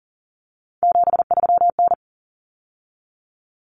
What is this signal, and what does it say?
Morse code, heard as a keyed tone, about 40 wpm, 700 Hz. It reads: 73N
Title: Common Cut Numbers - 3-Digit 40wpm